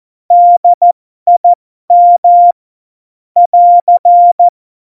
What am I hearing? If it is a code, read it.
Morse code, heard as a keyed tone, about 14 wpm, spaced as for 12 wpm, 700 Hz. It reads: DIM <AR>
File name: Morsekurs / Witze